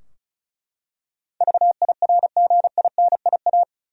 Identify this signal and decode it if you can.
Morse code, heard as a keyed tone, about 36 wpm, 700 Hz. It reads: VIRGINIA
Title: USA Common Names - 35wpm